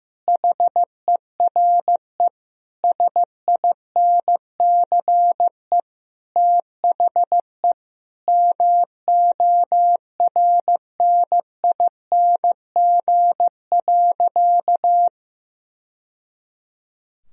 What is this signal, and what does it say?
Morse code, heard as a keyed tone, about 16 wpm, 700 Hz. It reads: HERE SINCE THE MORNING.
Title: Sentences from Top 1000 Words - Review Part 1 - 15wpm